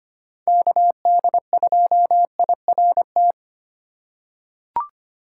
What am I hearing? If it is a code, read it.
Morse code, heard as a keyed tone, about 25 wpm, 700 Hz. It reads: KD2IRT
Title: USA Call Signs - 6 Characters - 4x Character Spacing - 25wpm